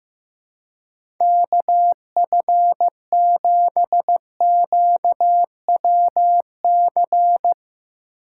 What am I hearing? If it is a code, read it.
Morse code, heard as a keyed tone, about 15 wpm, 700 Hz. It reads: KF7QWC
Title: USA Call Signs - 6 Characters - 4x Character Spacing - 15wpm